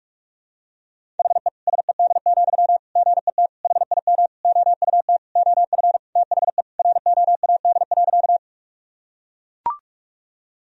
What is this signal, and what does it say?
Morse code, heard as a keyed tone, about 43 wpm, 700 Hz. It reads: HE SED, GET HIM OUT OF THE ROAD.
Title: Sentences from Top 400 Words and Abbreviations 45wpm